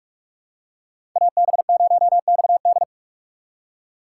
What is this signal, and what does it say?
Morse code, heard as a keyed tone, about 46 wpm, 700 Hz. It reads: AB0XD
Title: USA Call Signs - 2x Character Spacing - 45wpm